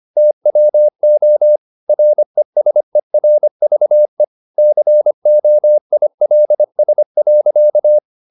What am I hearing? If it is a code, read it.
Morse code, heard as a keyed tone, about 25 wpm, 600 Hz. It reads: TWO RESERVE COILS.
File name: the old man and the sea part 1 25wpm